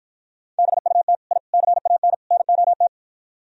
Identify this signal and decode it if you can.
Morse code, heard as a keyed tone, about 52 wpm, 700 Hz. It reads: BUT I CAN NOT